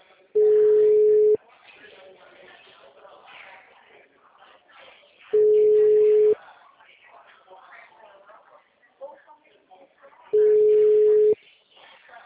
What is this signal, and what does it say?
Ringback tone (Europe, 425 Hz), with speech.